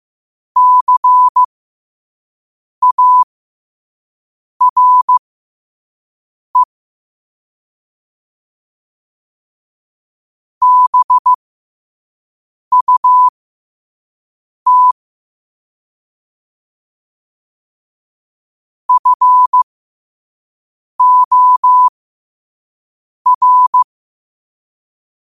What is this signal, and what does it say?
Morse code, heard as a keyed tone, about 15 wpm, spaced as for 5 wpm, 1000 Hz. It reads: CARE BUT FOR
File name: qotdfiles / 5wpm